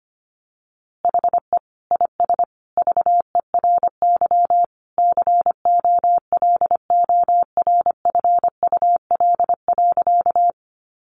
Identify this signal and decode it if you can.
Morse code, heard as a keyed tone, about 25 wpm, 700 Hz. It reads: HE IS VERY COLORFUL.
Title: Easy Phrases 25wpm